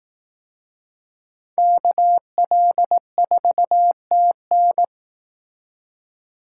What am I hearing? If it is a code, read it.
Morse code, heard as a keyed tone, about 18 wpm, 700 Hz. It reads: KL4TN
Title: Speed Racing - USA 2x2 Call Signs 15wpm